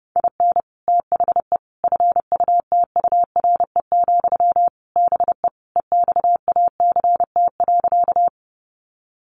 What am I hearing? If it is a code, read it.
Morse code, heard as a keyed tone, about 30 wpm, 700 Hz. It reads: IN THE FUTURE, BE EXACT.
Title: Sentences from Top 1300 Words 30wpm